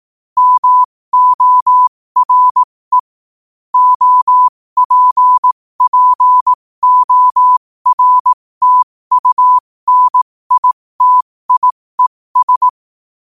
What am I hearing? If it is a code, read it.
Morse code, heard as a keyed tone, about 18 wpm, spaced as for 15 wpm, 1000 Hz. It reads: MORE OPPORTUNITIES
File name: qotdfiles / 15wpm